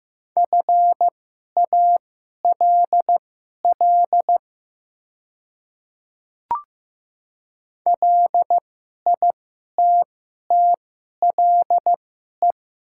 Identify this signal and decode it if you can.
Morse code, heard as a keyed tone, about 15 wpm, spaced as for 11 wpm, 700 Hz. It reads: FALL LITTLE
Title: 2x Character Spacing - Top 500 Words - 15wpm